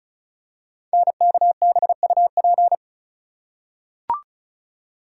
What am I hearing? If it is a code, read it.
Morse code, heard as a keyed tone, about 36 wpm, 700 Hz. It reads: NKBUP
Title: Code Groups - 5 Letters - 35wpm